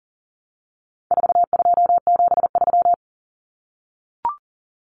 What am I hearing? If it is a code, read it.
Morse code, heard as a keyed tone, about 40 wpm, 700 Hz. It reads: 4273
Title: Numbers - 4-Digit 40wpm